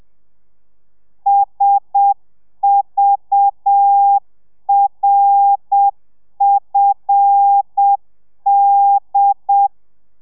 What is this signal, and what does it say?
Morse code, heard as a keyed tone, about 7 wpm, 800 Hz. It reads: SVRFD